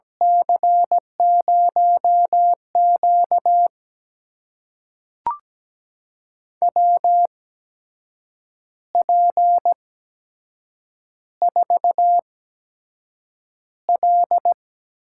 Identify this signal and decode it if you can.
Morse code, heard as a keyed tone, about 17 wpm, spaced as for 7 wpm, 700 Hz. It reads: C0Q WP4L
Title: USA Call Signs - 4 Characters - 8x Character Spacing - 17wpm